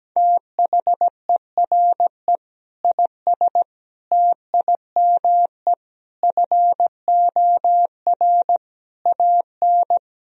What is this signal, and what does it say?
Morse code, heard as a keyed tone, about 17 wpm, 700 Hz. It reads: THERE IS TIME FOR AN
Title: Sentences from Top 1000 Words - Review Part 3 - 17wpm